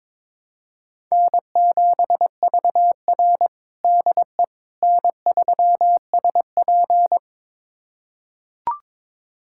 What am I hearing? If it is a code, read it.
Morse code, heard as a keyed tone, about 22 wpm, 700 Hz. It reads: N7VR DE N3SP
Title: USA Callsign DE Callsign - 4 Characters - 22wpm